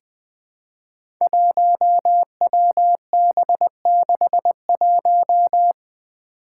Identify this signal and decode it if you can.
Morse code, heard as a keyed tone, about 20 wpm, 700 Hz. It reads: 1WB61
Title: Five Challenging Letter-Number Combinations 20wpm